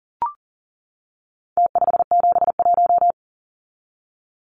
Morse code, heard as a keyed tone, about 40 wpm, 700 Hz. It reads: T571
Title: Numbers - Contest Style - 4-Digit 40wpm